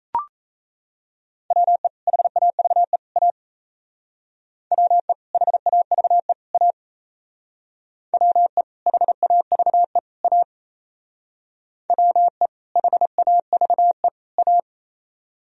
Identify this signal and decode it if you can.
Morse code, heard as a keyed tone, about 36 wpm, 700 Hz. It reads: WE HAVE A WE HAVE A WE HAVE A WE HAVE A